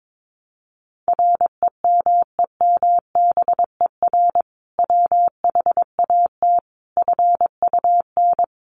Morse code, heard as a keyed tone, about 22 wpm, 700 Hz. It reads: REMEMBER WHAT FUN
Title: Sentences from Top 700 Words 22wpm